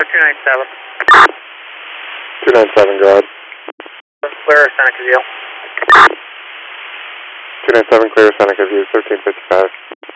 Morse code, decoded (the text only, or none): none